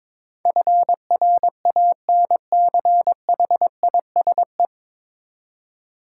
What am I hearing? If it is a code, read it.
Morse code, heard as a keyed tone, about 22 wpm, 700 Hz. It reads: FRANCHISE
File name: practice / 22wpm / 9 Letter Words - First 3 Letters Given - 22wpm